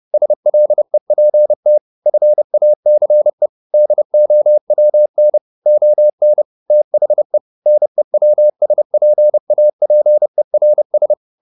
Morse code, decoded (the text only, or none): SLEPT FACE DOWN ON THE NEWSPAPERS